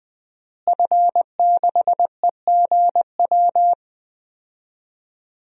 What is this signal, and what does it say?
Morse code, heard as a keyed tone, about 20 wpm, 700 Hz. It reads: F6EGW
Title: DXCC-100 Call Signs - 5 Characters - Repeated 3x - 20wpm